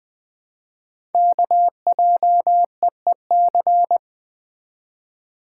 Morse code, heard as a keyed tone, about 20 wpm, 700 Hz. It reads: KJEEC
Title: Code Groups - 5 Letters - 20wpm